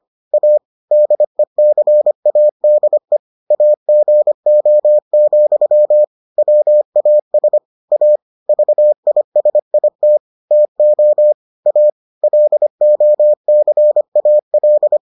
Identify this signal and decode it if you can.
Morse code, heard as a keyed tone, about 25 wpm, 600 Hz. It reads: A DECADE AGO, WAS A VISIT TO A LOCAL